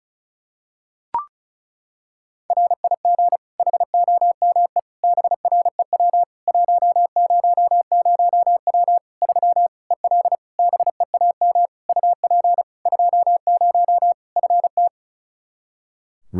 Morse code, heard as a keyed tone, about 36 wpm, 700 Hz. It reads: RIG HOME BREW 100W 3 EL BEAM UP 20 FT
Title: Practice QSO - Exchange 2 - Info - 35wpm